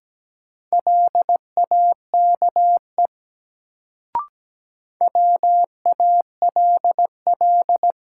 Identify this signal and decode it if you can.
Morse code, heard as a keyed tone, about 17 wpm, 700 Hz. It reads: LAKE WALL
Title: Sentences from Top 1100 Words - New Word Drill Alt - 17wpm